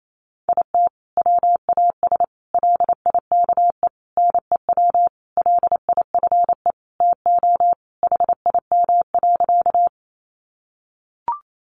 Morse code, heard as a keyed tone, about 28 wpm, 700 Hz. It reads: IT WAS LIKE NEW LIFE TO HIM.
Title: Sentences from Top 300 Words 28wpm